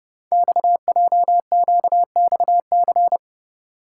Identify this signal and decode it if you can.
Morse code, heard as a keyed tone, about 28 wpm, 700 Hz. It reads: XJQXC
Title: Code Groups - 5 Letters - 30wpm